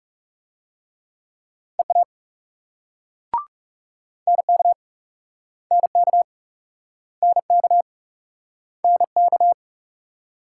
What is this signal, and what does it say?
Morse code, heard as a keyed tone, about 39 wpm, 700 Hz. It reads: EA NK NK NK NK